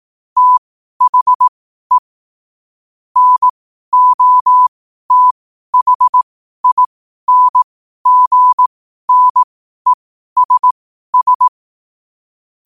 Morse code, heard as a keyed tone, about 18 wpm, spaced as for 12 wpm, 1000 Hz. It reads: THE NOTHINGNESS